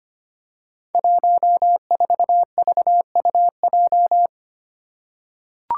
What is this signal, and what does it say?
Morse code, heard as a keyed tone, about 25 wpm, 700 Hz. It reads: 14VUJ
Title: Five Challenging Letter-Number Combinations 25wpm